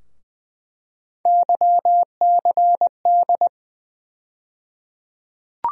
Morse code, heard as a keyed tone, about 20 wpm, 700 Hz. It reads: YCD
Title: USA Call Sign Postfixes - 3 Letters 20wpm